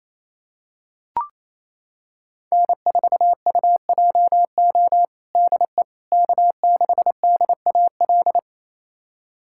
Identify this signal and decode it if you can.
Morse code, heard as a keyed tone, about 28 wpm, 700 Hz. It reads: N4UJO DE K6DAL
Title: USA Callsign DE Callsign - 5 Characters - 28wpm